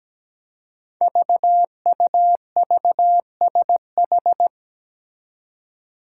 Morse code, heard as a keyed tone, about 17 wpm, 700 Hz. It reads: VUVSH
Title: Five Challenging Letter-Number Combinations 17wpm